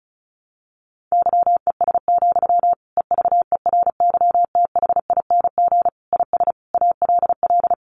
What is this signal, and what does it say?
Morse code, heard as a keyed tone, about 34 wpm, 700 Hz. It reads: YES, EVERYTHING IS ALL
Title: Sentences from Top 1100 Words 35wpm